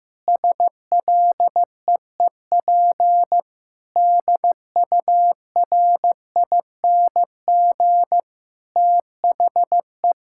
Morse code, heard as a keyed tone, about 15 wpm, 700 Hz. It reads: SLEEP DURING THE
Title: Sentences from Top 1000 Words - Review Part 6 - 15wpm